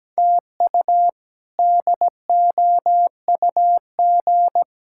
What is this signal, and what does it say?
Morse code, heard as a keyed tone, about 17 wpm, 700 Hz. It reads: TU DOUG
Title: K1USN SST QSO Practice - 17wpm